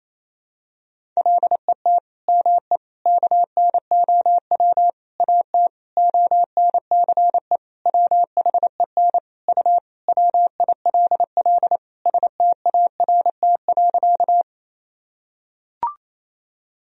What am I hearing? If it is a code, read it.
Morse code, heard as a keyed tone, about 28 wpm, 700 Hz. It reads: LET ME KNOW AT ONCE WHEN U WILL START.
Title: Sentences from Top 300 Words and Abbreviations 28wpm